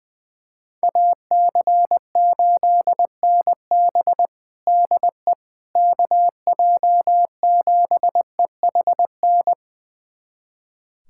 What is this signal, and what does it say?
Morse code, heard as a keyed tone, about 21 wpm, 700 Hz. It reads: AC8NB DE KJ7EHN
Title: USA Callsign DE Callsign 20wpm